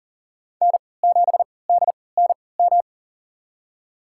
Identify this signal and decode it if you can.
Morse code, heard as a keyed tone, about 40 wpm, spaced as for 23 wpm, 700 Hz. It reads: N7DNM
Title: USA Call Signs - 5 Characters - 3x Character Spacing - 40wpm